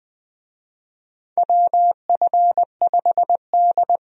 Morse code, heard as a keyed tone, about 20 wpm, 700 Hz. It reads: WF5D
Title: USA Call Signs - 4 Characters - 20wpm